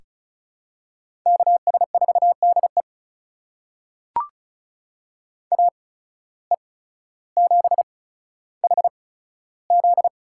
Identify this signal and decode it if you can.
Morse code, heard as a keyed tone, about 34 wpm, spaced as for 15 wpm, 700 Hz. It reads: KS4DE AE7HZ
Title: USA Call Signs - 5 Characters - 8x Character Spacing - 35wpm